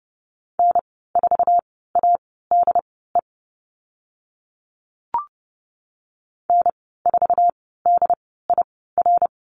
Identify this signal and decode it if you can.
Morse code, heard as a keyed tone, about 30 wpm, spaced as for 17 wpm, 700 Hz. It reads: N4ADE N4DIR
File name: Lesson 15g - Number 5 - Call Signs 30wpm